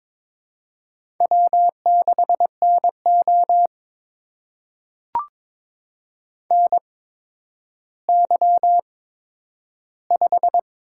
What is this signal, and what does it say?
Morse code, heard as a keyed tone, about 22 wpm, spaced as for 11 wpm, 700 Hz. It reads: W6NO NY5
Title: USA Call Signs - 4 Characters - 8x Character Spacing - 22wpm